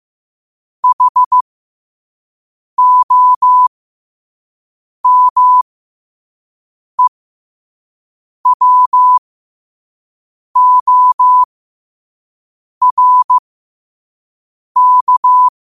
Morse code, heard as a keyed tone, about 15 wpm, spaced as for 5 wpm, 1000 Hz. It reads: HOMEWORK
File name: qotdfiles / 5wpm